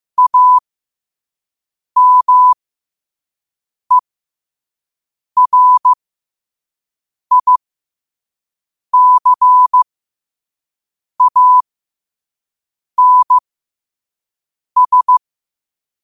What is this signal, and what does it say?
Morse code, heard as a keyed tone, about 15 wpm, spaced as for 5 wpm, 1000 Hz. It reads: AMERICANS